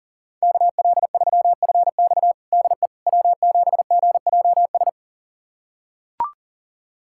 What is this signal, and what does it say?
Morse code, heard as a keyed tone, about 37 wpm, 700 Hz. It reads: KL3FX DE W7GJS
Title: USA Callsign DE Callsign - 5 Characters - 40wpm